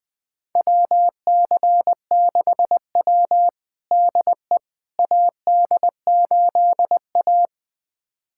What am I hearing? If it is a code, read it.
Morse code, heard as a keyed tone, about 20 wpm, 700 Hz. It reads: WC6W DE AD8A
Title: USA Callsign DE Callsign - 4 Characters - 20wpm